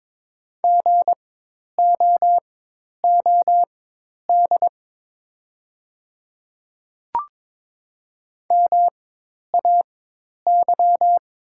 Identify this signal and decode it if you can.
Morse code, heard as a keyed tone, about 22 wpm, spaced as for 10 wpm, 700 Hz. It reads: GOOD MAY
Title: 4x Character Spacing - Top 200 Words - 22wpm